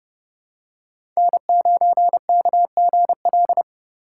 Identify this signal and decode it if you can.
Morse code, heard as a keyed tone, about 30 wpm, 700 Hz. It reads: N9KGL
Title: USA Call Signs - 2x Character Spacing - 30wpm